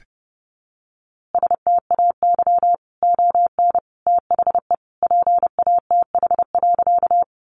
Morse code, heard as a keyed tone, about 28 wpm, 700 Hz. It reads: STAY ON THE PATH.
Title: A Sound Of Thunder 30wpm